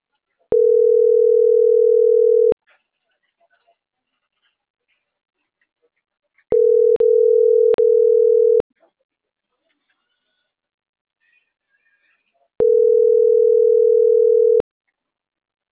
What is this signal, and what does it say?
Ringback tone (North America), 440 + 480 Hz.